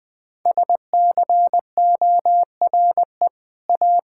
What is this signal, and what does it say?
Morse code, heard as a keyed tone, about 20 wpm, 700 Hz. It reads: SCORE A
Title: Sentences from Top 1600 Words 20wpm